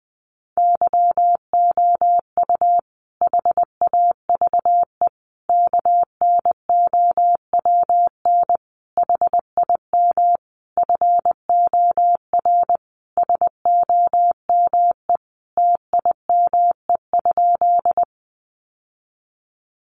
Morse code, heard as a keyed tone, about 20 wpm, 700 Hz. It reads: YOU HAVE KNOWN HIM FOR SOME TIME?
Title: Sentences from Top 600 Words 20wpm